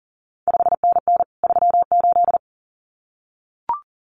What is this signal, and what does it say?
Morse code, heard as a keyed tone, about 40 wpm, 700 Hz. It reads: 5NN 38
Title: CQ WW DX Contest Exchange Encore - 40wpm